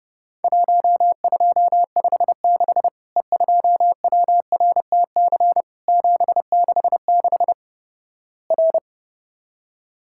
Morse code, heard as a keyed tone, about 30 wpm, 700 Hz. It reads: 1256 E2WRTC 766 R
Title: Worked All Europe DX Contest - QTC Exchange - 30wpm